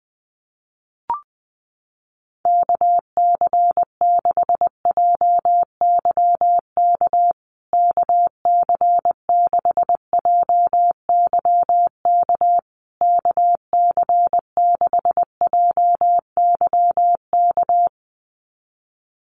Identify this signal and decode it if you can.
Morse code, heard as a keyed tone, about 20 wpm, 700 Hz. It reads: KC6JYK KC6JYK KC6JYK
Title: USA Call Signs Repeated 3x 20wpm